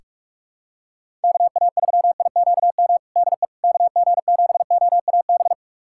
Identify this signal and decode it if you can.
Morse code, heard as a keyed tone, about 46 wpm, 700 Hz. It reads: KA3IQM DE KG7OAB
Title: USA Callsign DE Callsign - 6 Characters - 45wpm